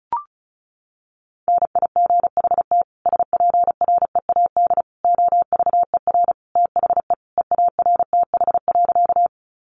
Morse code, heard as a keyed tone, about 34 wpm, 700 Hz. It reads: NIGHT SPREAD OVER THE EARTH.
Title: Sentences from Top 1200 Words 35wpm